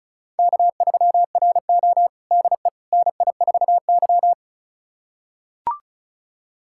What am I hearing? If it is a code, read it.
Morse code, heard as a keyed tone, about 34 wpm, 700 Hz. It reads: K3RO DE NI4Y
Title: USA Callsign DE Callsign - 4 Characters - 35wpm